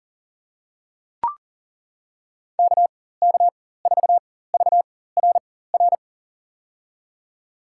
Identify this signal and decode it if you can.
Morse code, heard as a keyed tone, about 40 wpm, spaced as for 19 wpm, 700 Hz. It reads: KK4VRR
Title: USA Call Signs - 6 Characters - 4x Character Spacing - 40wpm